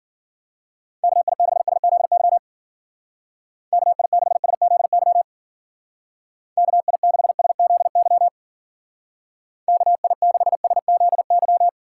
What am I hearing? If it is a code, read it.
Morse code, heard as a keyed tone, about 49 wpm, 700 Hz. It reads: KI6SZY KI6SZY KI6SZY KI6SZY
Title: Speed Racing - USA 2x3 Call Signs 40wpm